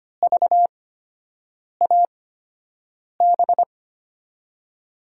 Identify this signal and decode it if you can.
Morse code, heard as a keyed tone, about 25 wpm, spaced as for 7 wpm, 700 Hz. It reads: VAB
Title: USA Call Signs - 5 Characters - 8x Character Spacing - 25wpm